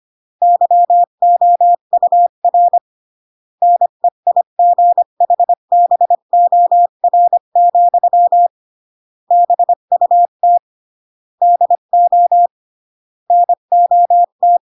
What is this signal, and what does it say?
Morse code, heard as a keyed tone, about 25 wpm, 700 Hz. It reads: YOUR NEIGHBOR, BUT DO NOT